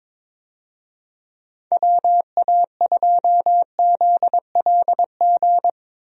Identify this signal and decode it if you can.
Morse code, heard as a keyed tone, about 22 wpm, 700 Hz. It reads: WA2ZLG
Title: USA Call Signs - 6 Characters - Repeated 3x - 22wpm